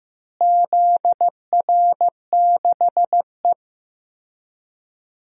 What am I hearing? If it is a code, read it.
Morse code, heard as a keyed tone, about 15 wpm, 700 Hz. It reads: ZR6E
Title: DXCC-100 Call Signs - 4 Characters - Repeated 3x - 15wpm